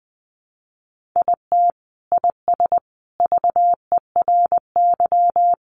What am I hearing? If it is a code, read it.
Morse code, heard as a keyed tone, about 20 wpm, 700 Hz. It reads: IT IS VERY
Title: Sentences from Top 200 Words 20wpm